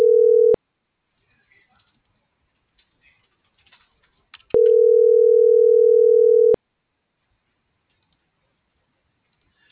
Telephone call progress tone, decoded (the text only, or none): ringback tone